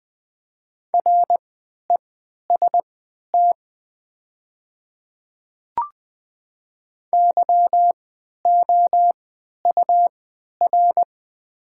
Morse code, heard as a keyed tone, about 20 wpm, spaced as for 11 wpm, 700 Hz. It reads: REST YOUR
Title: 3x Character Spacing - Top 500 Words - 20wpm